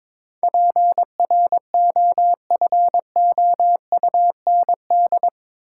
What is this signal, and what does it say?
Morse code, heard as a keyed tone, about 21 wpm, 700 Hz. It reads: PROFOUND
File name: practice / 22wpm / Eight Letter Words - 22wpm